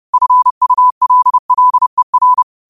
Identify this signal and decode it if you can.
Morse code, heard as a keyed tone, about 30 wpm, 1000 Hz. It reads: FULLER
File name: qotdfiles / 30wpm